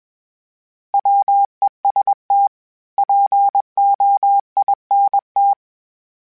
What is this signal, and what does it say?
Morse code, heard as a keyed tone, about 20 wpm, 800 Hz. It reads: WEST POINT
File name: fromSMs / FromSMs mp3s / Cities@20